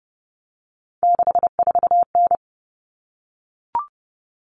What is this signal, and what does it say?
Morse code, heard as a keyed tone, about 30 wpm, 700 Hz. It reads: 64N
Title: All Cut Numbers - 3-Digit 30wpm